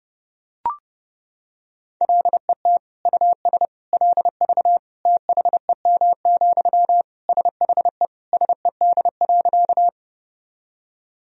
Morse code, heard as a keyed tone, about 32 wpm, 700 Hz. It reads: LET US LV THEM, SHE SED.
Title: Sentences from Top 400 Words and Abbreviations 30wpm